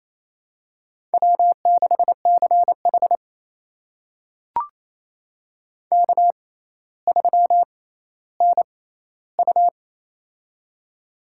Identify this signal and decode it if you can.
Morse code, heard as a keyed tone, about 28 wpm, spaced as for 14 wpm, 700 Hz. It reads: W6CH K3NU
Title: USA Call Signs - 4 Characters - 6x Character Spacing - 28wpm